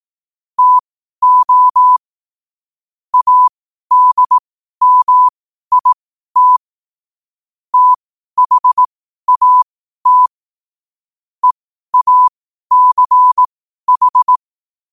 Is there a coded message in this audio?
Yes: Morse code from 0.6 to 14.4 s, heard as a keyed tone, about 18 wpm, spaced as for 12 wpm, 1000 Hz. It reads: TO ADMIT THAT EACH